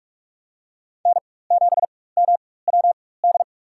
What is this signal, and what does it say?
Morse code, heard as a keyed tone, about 47 wpm, spaced as for 21 wpm, 700 Hz. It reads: N7MWD
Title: USA Call Signs - 5 Characters - 4x Character Spacing - 45wpm